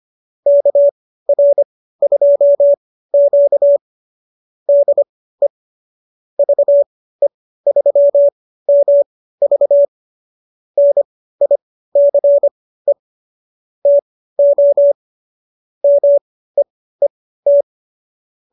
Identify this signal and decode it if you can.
Morse code, heard as a keyed tone, about 25 wpm, spaced as for 15 wpm, 590 Hz. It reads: KR2Q DE VE3MV NICE TO MEET